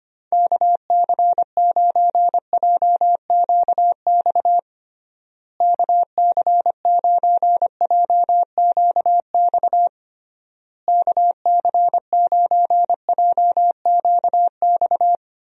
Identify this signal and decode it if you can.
Morse code, heard as a keyed tone, about 25 wpm, 700 Hz. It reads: KC9JQX KC9JQX KC9JQX
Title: USA Call Signs - 6 Characters - Repeated 3x - 25wpm